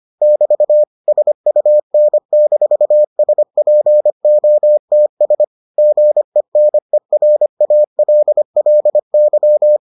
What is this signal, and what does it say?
Morse code, heard as a keyed tone, about 25 wpm, 600 Hz. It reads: <BT> SUN-SPOTS GENERALLY